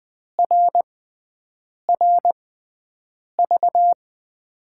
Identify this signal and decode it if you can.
Morse code, heard as a keyed tone, about 20 wpm, spaced as for 7 wpm, 700 Hz. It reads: RRV
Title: USA Call Signs - 6x Character Spacing - 20wpm